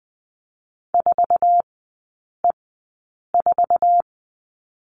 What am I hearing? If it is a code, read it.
Morse code, heard as a keyed tone, about 20 wpm, spaced as for 8 wpm, 700 Hz. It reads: 4E4
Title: 047 Lesson 9e - Number 4 - Three Characters 20wpm